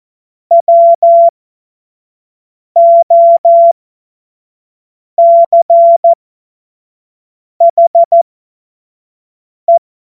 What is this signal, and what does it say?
Morse code, heard as a keyed tone, about 14 wpm, spaced as for 5 wpm, 680 Hz. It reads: WOCHE
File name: Morsekurs / Witze / Witz 16. Dezember 2025 25 F70